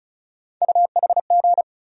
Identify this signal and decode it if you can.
Morse code, heard as a keyed tone, about 36 wpm, 700 Hz. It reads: UHG
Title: USA Call Sign Postfixes - 3 Letters 35wpm